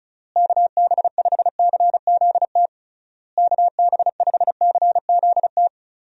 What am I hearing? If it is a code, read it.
Morse code, heard as a keyed tone, about 34 wpm, 700 Hz. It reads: KB5CZT KB5CZT
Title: USA Call Signs - 6 Characters - Repeated 3x - 35wpm